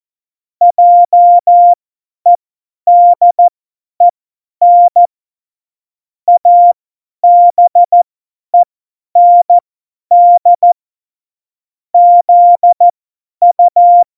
Morse code, heard as a keyed tone, about 14 wpm, spaced as for 10 wpm, 700 Hz. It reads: JEDEN ABEND ZU
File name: Morsekurs / Witze / Witz 11. Februar 2020, 50 F70